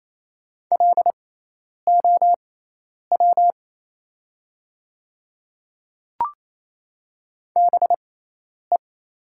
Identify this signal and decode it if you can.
Morse code, heard as a keyed tone, about 28 wpm, spaced as for 10 wpm, 700 Hz. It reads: LOW BE
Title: 6x Character Spacing - Top 300 Words - 28wpm